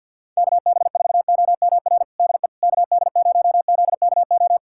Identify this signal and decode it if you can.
Morse code, heard as a keyed tone, about 53 wpm, 700 Hz. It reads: KB4OMR DE KD0ZKO
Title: USA Callsign DE Callsign - 6 Characters - 50wpm